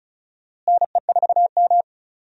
Morse code, heard as a keyed tone, about 34 wpm, 700 Hz. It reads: NE4M
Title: USA Call Signs - 4 Characters - Repeated 3x - 35wpm